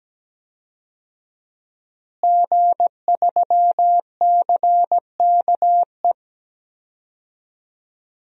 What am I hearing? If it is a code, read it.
Morse code, heard as a keyed tone, about 17 wpm, 700 Hz. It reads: G3CKE
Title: DXCC-100 Call Signs - 5 Characters - Repeated 2x - 17wpm